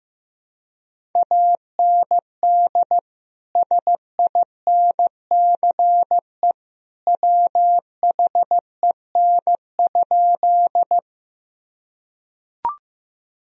Morse code, heard as a keyed tone, about 15 wpm, 700 Hz. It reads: AND SINCE WHEN?
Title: Sentences from Top 1000 Words - Review Part 1 - 15wpm